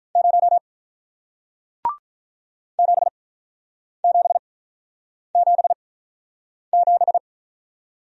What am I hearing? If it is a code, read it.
Morse code, heard as a keyed tone, about 48 wpm, spaced as for 9 wpm, 700 Hz. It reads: 0 7777